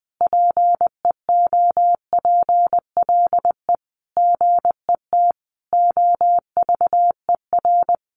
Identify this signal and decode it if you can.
Morse code, heard as a keyed tone, about 20 wpm, 700 Hz. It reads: PEOPLE GET OVER
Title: Sentences from Top 900 Words 20wpm